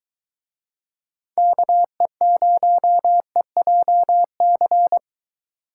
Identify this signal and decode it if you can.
Morse code, heard as a keyed tone, about 22 wpm, 700 Hz. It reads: KE0EJC